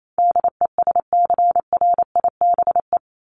Morse code, heard as a keyed tone, about 28 wpm, 700 Hz. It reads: DESCRIBE
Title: Top 751-1000 Words 28wpm